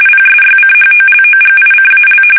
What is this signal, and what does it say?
RTTY, about 45.45 baud. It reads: PSYCHOLOGICAL
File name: fsk850 / battleshipnj